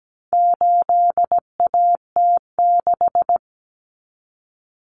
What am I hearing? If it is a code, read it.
Morse code, heard as a keyed tone, about 17 wpm, 700 Hz. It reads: 8AT6
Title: Common Cut Numbers - 4-Digit 17wpm